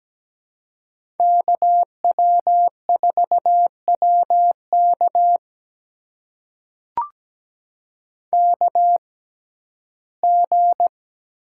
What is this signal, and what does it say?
Morse code, heard as a keyed tone, about 17 wpm, spaced as for 12 wpm, 700 Hz. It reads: KW4WK KG
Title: USA Call Signs - 5 Characters - 6x Character Spacing - 17wpm